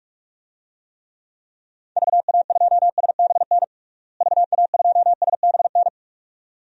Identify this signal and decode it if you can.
Morse code, heard as a keyed tone, about 46 wpm, 700 Hz. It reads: VA2SBN VA2SBN
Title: DXCC-100 Call Signs - 6 Characters - Repeated 2x - 45wpm